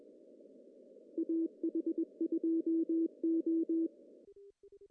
Morse code, heard as a keyed tone, about 22 wpm, 330 Hz. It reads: AH2O
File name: CW / cqww-cw-k5tr